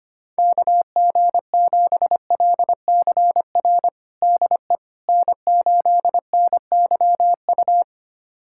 Morse code, heard as a keyed tone, about 25 wpm, 700 Hz. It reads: KG7LCR DE N8NYU
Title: USA Callsign DE Callsign 25wpm